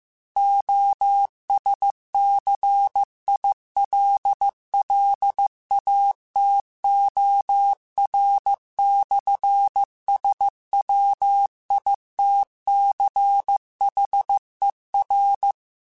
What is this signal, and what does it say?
Morse code, heard as a keyed tone, about 15 wpm, 800 Hz. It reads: OSCILLATOR/SWITCHER
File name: Technical Subjects / Technical Subjects-mp3s / Sw Pwr Sup @13